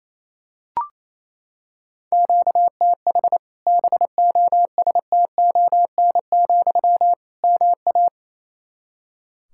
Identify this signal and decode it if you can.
Morse code, heard as a keyed tone, about 28 wpm, 700 Hz. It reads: QTH BOSTON, MA
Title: Practice QSO - Exchange 1 - Beginning - 28wpm